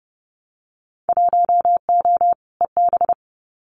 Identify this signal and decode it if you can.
Morse code, heard as a keyed tone, about 30 wpm, 700 Hz. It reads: 1O EB
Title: Winter Field Day Exchanges 30wpm